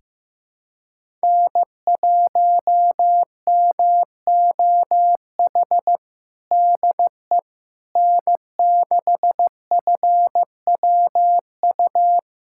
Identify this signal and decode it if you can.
Morse code, heard as a keyed tone, about 15 wpm, 700 Hz. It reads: N1MOH DE N6FWU
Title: USA Callsign DE Callsign - 5 Characters - 15wpm